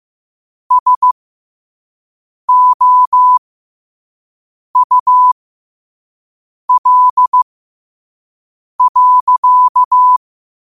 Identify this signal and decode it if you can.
Morse code, heard as a keyed tone, about 15 wpm, spaced as for 5 wpm, 1000 Hz. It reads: SOUL.